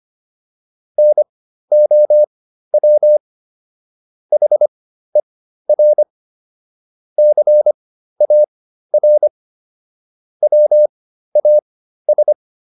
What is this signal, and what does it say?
Morse code, heard as a keyed tone, about 25 wpm, spaced as for 13 wpm, 600 Hz. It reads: NOW HER CAR WAS